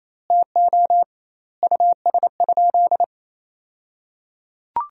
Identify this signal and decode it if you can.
Morse code, heard as a keyed tone, about 28 wpm, 700 Hz. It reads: TO US?
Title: 2x Word-Spacing - Sentences from Top 400 Words - 28wpm